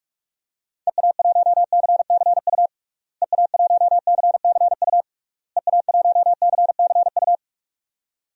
Morse code, heard as a keyed tone, about 41 wpm, 700 Hz. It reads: EA1CCU EA1CCU EA1CCU
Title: DXCC-100 Call Signs - 6 Characters - Repeated 3x - 45wpm